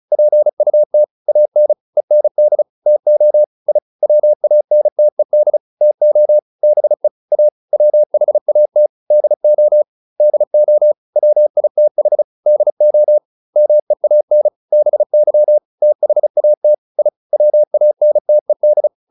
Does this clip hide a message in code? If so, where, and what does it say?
Yes: Morse code from 0.1 to 18.9 s, heard as a keyed tone, about 34 wpm, 600 Hz. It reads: PUT AN END TO I WANTED TO BE A WHAT DO DO WITH DO MEAN BY THAT I WANTED